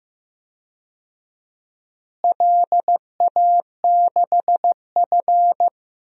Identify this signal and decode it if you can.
Morse code, heard as a keyed tone, about 15 wpm, 700 Hz. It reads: LA6F
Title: DXCC-100 Call Signs - 4 Characters - Repeated 3x - 15wpm